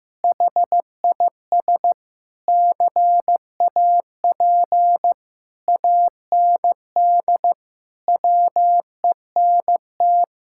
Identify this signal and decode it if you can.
Morse code, heard as a keyed tone, about 16 wpm, 700 Hz. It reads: HIS CAP AND WENT
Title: Sentences from Top 1600 Words 15wpm